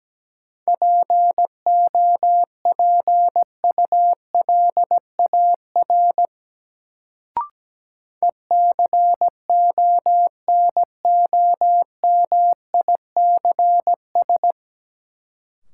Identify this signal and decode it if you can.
Morse code, heard as a keyed tone, about 17 wpm, 700 Hz. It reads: POPULAR ECONOMICS
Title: Sentences from Top 1000 Words - Review Part 3 - New Word Drill - 17wpm